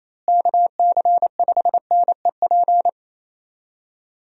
Morse code, heard as a keyed tone, about 28 wpm, 700 Hz. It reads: KC5NEP
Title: USA Call Signs - 6 Characters - Repeated 2x - 28wpm